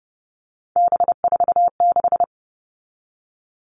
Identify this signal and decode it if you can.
Morse code, heard as a keyed tone, about 30 wpm, 700 Hz. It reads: B46
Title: All Cut Numbers - 3-Digit 30wpm